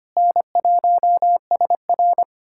Morse code, heard as a keyed tone, about 25 wpm, 700 Hz. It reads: N1SR
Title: USA Callsign DE Callsign - 4 Characters - 25wpm